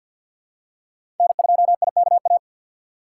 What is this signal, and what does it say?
Morse code, heard as a keyed tone, about 52 wpm, 700 Hz. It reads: N2IKA
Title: USA Call Signs - 5 Characters - 2x Character Spacing - 50wpm